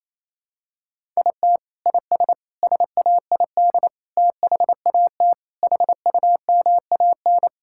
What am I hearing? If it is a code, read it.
Morse code, heard as a keyed tone, about 28 wpm, 700 Hz. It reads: IT IS SAID THAT HUMAN
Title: Sentences from Top 1000 Words - Review Part 6 - 28wpm